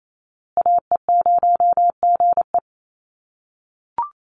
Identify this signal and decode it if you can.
Morse code, heard as a keyed tone, about 28 wpm, 700 Hz. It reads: AE0GE
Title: USA Call Signs Repeated 2x 28wpm